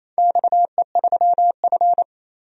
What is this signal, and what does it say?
Morse code, heard as a keyed tone, about 28 wpm, 700 Hz. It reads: XE3F
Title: DXCC-100 Call Signs - 4 Characters - 28wpm